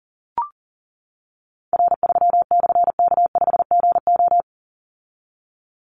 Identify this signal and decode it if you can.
Morse code, heard as a keyed tone, about 38 wpm, 700 Hz. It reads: R3/K5GO
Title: Call Signs - 40wpm